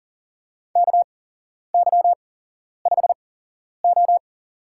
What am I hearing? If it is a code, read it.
Morse code, heard as a keyed tone, about 37 wpm, spaced as for 11 wpm, 700 Hz. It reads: KY5O
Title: USA Call Signs - 4 Characters - 8x Character Spacing - 40wpm